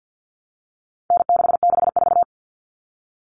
Morse code, heard as a keyed tone, about 50 wpm, 700 Hz. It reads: N664
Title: All Cut Numbers - 4-Digit 50wpm